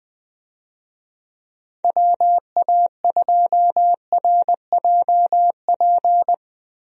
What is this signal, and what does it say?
Morse code, heard as a keyed tone, about 20 wpm, 700 Hz. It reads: WA2RJP
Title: USA Call Signs - 6 Characters - Repeated 2x - 20wpm